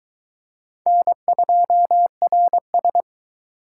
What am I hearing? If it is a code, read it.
Morse code, heard as a keyed tone, about 21 wpm, 700 Hz. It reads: N2RS